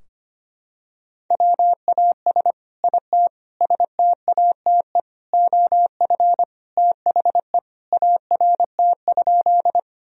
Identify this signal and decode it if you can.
Morse code, heard as a keyed tone, about 25 wpm, 700 Hz. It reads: WAS IT STATE OF THE ART?
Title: Sentences from Top 1000 Words - Review Part 3 - 25wpm